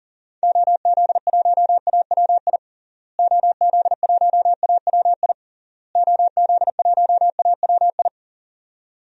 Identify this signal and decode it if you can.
Morse code, heard as a keyed tone, about 40 wpm, 700 Hz. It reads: OZ1AWI OZ1AWI OZ1AWI
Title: DXCC-100 Call Signs - 6 Characters - Repeated 3x - 40wpm